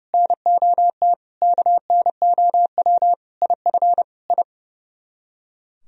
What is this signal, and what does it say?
Morse code, heard as a keyed tone, about 40 wpm, 700 Hz. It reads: NOT KNOW IF I